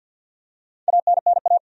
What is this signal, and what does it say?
Morse code, heard as a keyed tone, about 50 wpm, 700 Hz. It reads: ANNA
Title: USA Common Names - Max 4 Letters - 50wpm